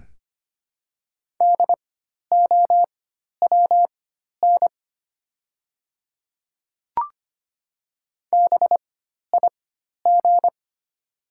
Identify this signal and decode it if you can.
Morse code, heard as a keyed tone, about 25 wpm, spaced as for 12 wpm, 700 Hz. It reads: DOWN BIG
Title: 4x Character Spacing - Top 200 Words - 25wpm